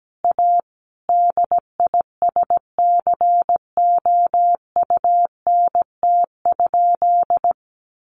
Morse code, heard as a keyed tone, about 17 wpm, 700 Hz. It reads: A DISCOUNT?
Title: Sentences from Top 1200 Words 17wpm